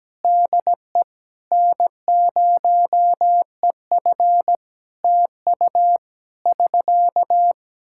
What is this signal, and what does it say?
Morse code, heard as a keyed tone, about 17 wpm, 700 Hz. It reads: DE N0EF TU <SK>
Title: Practice QSO - Exchange 4 - Ending - 17wpm